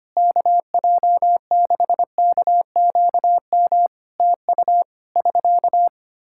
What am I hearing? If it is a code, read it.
Morse code, heard as a keyed tone, about 25 wpm, 700 Hz. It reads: KJ6KQM TU <SK>
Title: Practice QSO - Exchange 4 - Ending - 25wpm